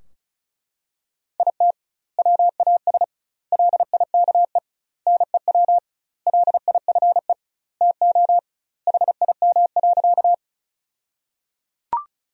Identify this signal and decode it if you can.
Morse code, heard as a keyed tone, about 34 wpm, 700 Hz. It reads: IT WAS LIKE NEW LIFE TO HIM.
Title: 2x Word-Spacing - Sentences from Top 300 Words - 35wpm